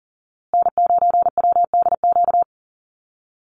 Morse code, heard as a keyed tone, about 38 wpm, 700 Hz. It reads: N9WDQ
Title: USA Call Signs Repeated 2x 40wpm